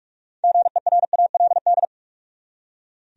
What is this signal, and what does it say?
Morse code, heard as a keyed tone, about 46 wpm, 700 Hz. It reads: GERALD
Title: USA Common Names - 45wpm